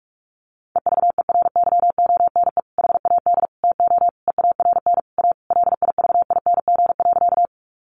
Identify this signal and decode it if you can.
Morse code, heard as a keyed tone, about 44 wpm, 700 Hz. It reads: EVERYONE HAD TO EARN A LIVING.
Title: Sentences from Top 1500 Words 45wpm